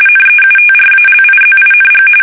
RTTY, about 45.45 baud. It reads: NEUTRALIZING
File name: fsk850 / USS-Massachusetts